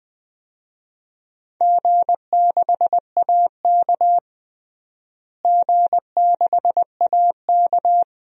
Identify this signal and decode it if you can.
Morse code, heard as a keyed tone, about 21 wpm, 700 Hz. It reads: G6AK G6AK
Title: DXCC-100 Call Signs - 4 Characters - Repeated 2x - 20wpm